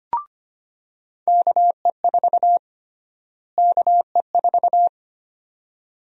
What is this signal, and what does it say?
Morse code, heard as a keyed tone, about 25 wpm, 700 Hz. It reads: KE4 KE4
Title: USA Call Sign Prefixes - Repeated 2x - 25wpm